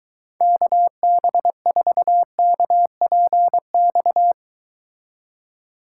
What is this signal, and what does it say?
Morse code, heard as a keyed tone, about 23 wpm, 700 Hz. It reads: KB4KPX